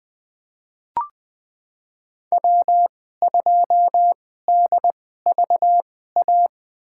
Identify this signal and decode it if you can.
Morse code, heard as a keyed tone, about 20 wpm, spaced as for 14 wpm, 700 Hz. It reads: W2DVA
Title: USA Call Signs - 5 Characters - 2x Character Spacing - 20wpm